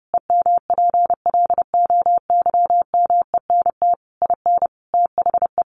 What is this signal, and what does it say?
Morse code, heard as a keyed tone, about 30 wpm, 700 Hz. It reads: EMPLOYMENT IN THE
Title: Sentences from Top 1500 Words 30wpm